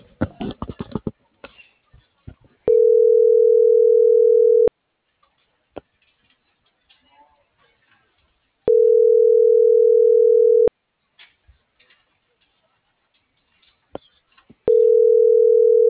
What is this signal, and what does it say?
Ringback tone (North America), 440 + 480 Hz.